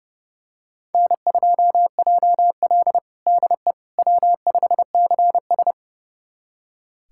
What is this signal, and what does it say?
Morse code, heard as a keyed tone, about 31 wpm, 700 Hz. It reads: N2JL DE W5CS
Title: USA Callsign DE Callsign - 4 Characters - 30wpm